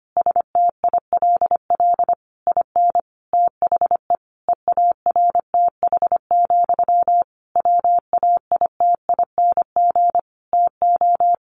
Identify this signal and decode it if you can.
Morse code, heard as a keyed tone, about 25 wpm, 700 Hz. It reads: STILL IN THE EARTH, WAITING TO
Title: A Sound Of Thunder 25wpm